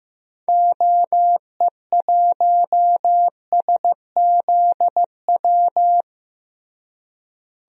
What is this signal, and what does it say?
Morse code, heard as a keyed tone, about 15 wpm, 700 Hz. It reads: OE1SZW
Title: DXCC-100 Call Signs - 6 Characters - Repeated 3x - 15wpm